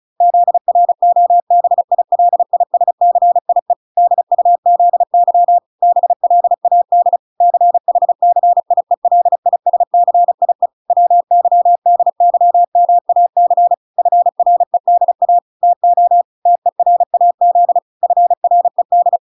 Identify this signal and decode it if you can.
Morse code, heard as a keyed tone, about 34 wpm, 700 Hz. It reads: ZROBILISCIE DUZY BLAD CHCIELISCIE WYDYMAC FREDA TO TERAZ FRED